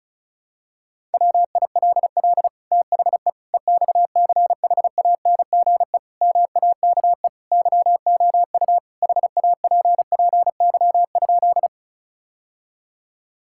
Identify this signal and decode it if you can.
Morse code, heard as a keyed tone, about 36 wpm, 700 Hz. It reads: WILL THE EXCHANGE MAKE YOU HAPPY?
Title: Sentences from Top 1000 Words - Review Part 6 - 35wpm